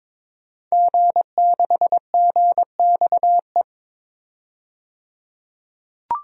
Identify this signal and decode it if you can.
Morse code, heard as a keyed tone, about 22 wpm, 700 Hz. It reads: G6GXE
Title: DXCC-100 Call Signs - 5 Characters - 22wpm